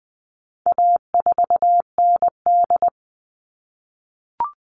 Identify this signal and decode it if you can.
Morse code, heard as a keyed tone, about 20 wpm, 700 Hz. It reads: A4ND
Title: All Cut Numbers - 4-Digit 20wpm